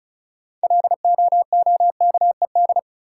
Morse code, heard as a keyed tone, about 34 wpm, 700 Hz. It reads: LOOKED
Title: Sentences from Top 800 Words - New Word Drill - 35wpm